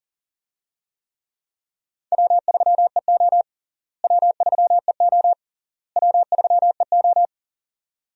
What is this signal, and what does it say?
Morse code, heard as a keyed tone, about 40 wpm, 700 Hz. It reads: W3EO W3EO W3EO
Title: USA Call Signs - 4 Characters - Repeated 3x - 40wpm